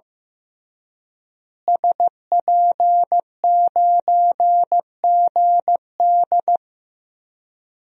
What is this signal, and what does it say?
Morse code, heard as a keyed tone, about 15 wpm, 700 Hz. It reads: SP9GD
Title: DXCC-100 Call Signs - 5 Characters - Repeated 3x - 15wpm